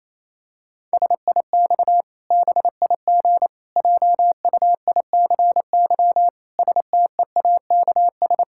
Morse code, heard as a keyed tone, about 28 wpm, 700 Hz. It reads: SIX BIG JUICY STEAKS
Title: Pangram Challenge v2 - 28wpm